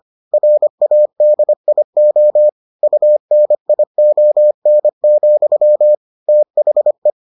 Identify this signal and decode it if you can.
Morse code, heard as a keyed tone, about 25 wpm, 600 Hz. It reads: RADIO UNION, THE